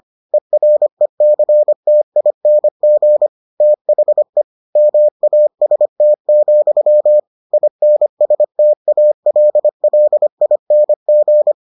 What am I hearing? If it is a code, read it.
Morse code, heard as a keyed tone, about 25 wpm, 600 Hz. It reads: ERECTING THE MAST, INSTALLING